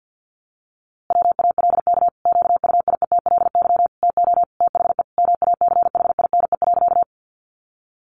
Morse code, heard as a keyed tone, about 50 wpm, 700 Hz. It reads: WALK QUIETLY TO THE MACHINE.
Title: A Sound Of Thunder 50wpm